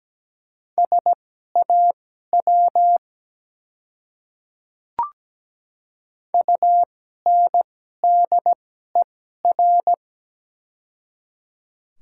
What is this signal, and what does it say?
Morse code, heard as a keyed tone, about 17 wpm, spaced as for 12 wpm, 690 Hz. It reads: SAW UNDER
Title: 2x Character Spacing - Top 300 Words - 17wpm